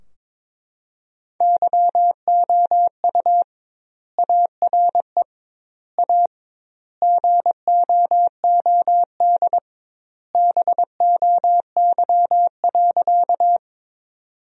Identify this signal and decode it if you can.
Morse code, heard as a keyed tone, about 22 wpm, 700 Hz. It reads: YOU ARE A GOOD BOY.
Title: 2x Word-Spacing - Sentences from Top 200 Words - 22wpm